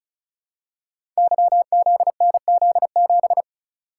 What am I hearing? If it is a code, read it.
Morse code, heard as a keyed tone, about 36 wpm, 700 Hz. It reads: YZNZ7
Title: Code Groups - 5 Letter:Numbers - 35wpm